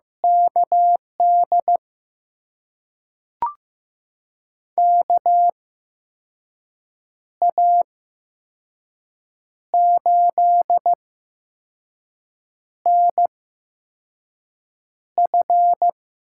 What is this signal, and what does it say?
Morse code, heard as a keyed tone, about 15 wpm, spaced as for 5 wpm, 700 Hz. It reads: KD KA8NF
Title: USA Call Signs - 6 Characters - 8x Character Spacing - 15wpm